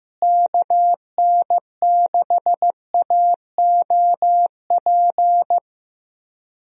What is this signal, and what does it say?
Morse code, heard as a keyed tone, about 15 wpm, 700 Hz. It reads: KN6AOP
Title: USA Call Signs - 6 Characters - Repeated 2x - 15wpm